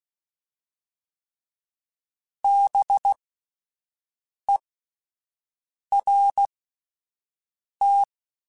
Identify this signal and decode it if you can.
Morse code, heard as a keyed tone, about 16 wpm, spaced as for 5 wpm, 800 Hz. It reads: BERT
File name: Spaced Alphas / Spaced Alphas mp3s / Bs sp @10